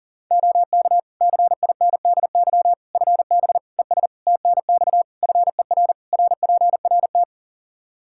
Morse code, heard as a keyed tone, about 40 wpm, 700 Hz. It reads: OK CINDY FB ES TNX FER RPRT
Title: K1USN SST QSO Practice 40wpm